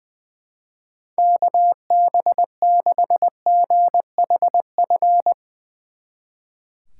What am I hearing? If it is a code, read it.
Morse code, heard as a keyed tone, about 20 wpm, 700 Hz. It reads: KB6GHF
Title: Speed Racing - USA 2x3 Call Signs 20wpm